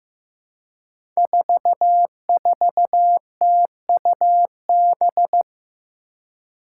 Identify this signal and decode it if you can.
Morse code, heard as a keyed tone, about 15 wpm, 700 Hz. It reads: 44TUB
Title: Five Challenging Letter-Number Combinations 15wpm